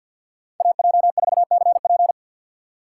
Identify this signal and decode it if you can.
Morse code, heard as a keyed tone, about 52 wpm, 700 Hz. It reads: AJ4CP
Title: USA Call Signs - 5 Characters - 2x Character Spacing - 50wpm